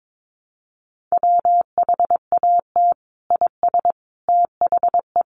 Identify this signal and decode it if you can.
Morse code, heard as a keyed tone, about 22 wpm, 700 Hz. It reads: WHAT IS THE
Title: Sentences from Top 1500 Words 22wpm